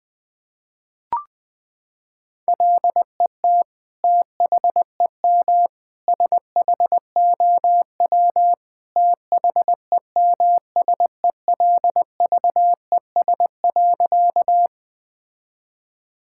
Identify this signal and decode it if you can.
Morse code, heard as a keyed tone, about 21 wpm, 700 Hz. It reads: LET THEM SHOW THEMSELVES.
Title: Sentences from Top 1000 Words - Review Part 5 - 20wpm